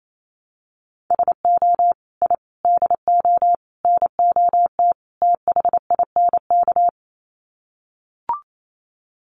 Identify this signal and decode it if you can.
Morse code, heard as a keyed tone, about 28 wpm, 700 Hz. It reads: SO I DO NOT THINK
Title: Sets of 5 Words 28wpm